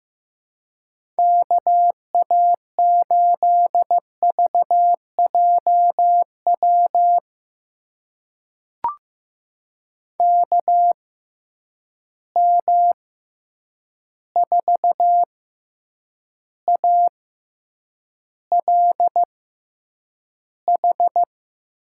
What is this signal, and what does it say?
Morse code, heard as a keyed tone, about 15 wpm, spaced as for 8 wpm, 700 Hz. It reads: KA8VJW KM4ALH
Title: USA Call Signs - 6 Characters - 6x Character Spacing - 15wpm